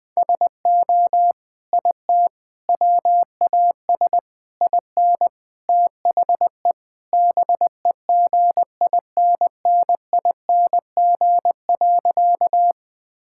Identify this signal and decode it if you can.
Morse code, heard as a keyed tone, about 19 wpm, 700 Hz. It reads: SO IT WAS IN THE BEGINNING.
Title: Sentences from Top 1000 Words - Review Part 5 - 20wpm